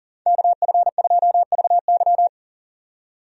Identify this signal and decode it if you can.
Morse code, heard as a keyed tone, about 40 wpm, 700 Hz. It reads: KF2VY
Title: USA Call Signs - 5 Characters - 40wpm